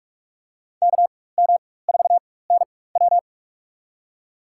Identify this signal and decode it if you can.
Morse code, heard as a keyed tone, about 44 wpm, spaced as for 21 wpm, 700 Hz. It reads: KM4NW
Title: USA Call Signs - 5 Characters - 4x Character Spacing - 45wpm